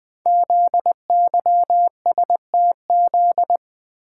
Morse code, heard as a keyed tone, about 20 wpm, 700 Hz. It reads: ZYSTZ
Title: Code Groups - 5 Letters - 20wpm